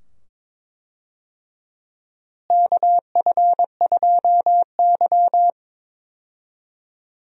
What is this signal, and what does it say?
Morse code, heard as a keyed tone, about 21 wpm, 700 Hz. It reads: KF2Y
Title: USA Call Signs - 4 Characters - Repeated 2x - 22wpm